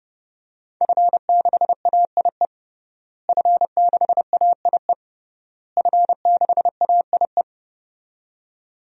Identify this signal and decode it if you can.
Morse code, heard as a keyed tone, about 32 wpm, 700 Hz. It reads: F6AIE F6AIE F6AIE
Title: DXCC-100 Call Signs - 5 Characters - Repeated 3x - 30wpm